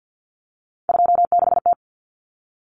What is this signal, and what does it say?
Morse code, heard as a keyed tone, about 50 wpm, 700 Hz. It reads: 26T
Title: Common Cut Numbers - 3-Digit 50wpm